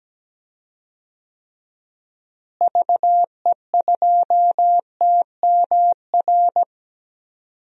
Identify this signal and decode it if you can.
Morse code, heard as a keyed tone, about 17 wpm, 700 Hz. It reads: VE2TMR
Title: DXCC-100 Call Signs - 6 Characters - Repeated 3x - 17wpm